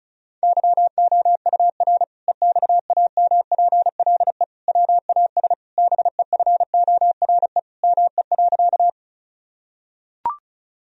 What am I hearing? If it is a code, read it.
Morse code, heard as a keyed tone, about 36 wpm, 700 Hz. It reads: YOUR EXAMPLE WAS BEFORE ME.
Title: Sentences from Top 1000 Words - Review Part 1 - 35wpm